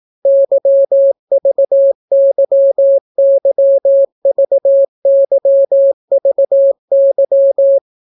Morse code, heard as a keyed tone, about 18 wpm, 550 Hz. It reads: YVYYVYVY